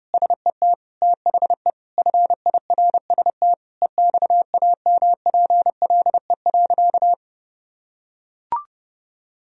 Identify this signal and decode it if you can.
Morse code, heard as a keyed tone, about 32 wpm, 700 Hz. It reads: SET THE FIRST EXAMPLE.
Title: Sentences from Top 1000 Words - Review Part 1 - 30wpm